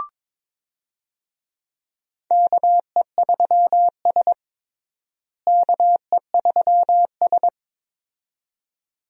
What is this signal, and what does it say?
Morse code, heard as a keyed tone, about 22 wpm, 700 Hz. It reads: KE3S KE3S
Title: USA Call Signs - 4 Characters - Repeated 2x - 22wpm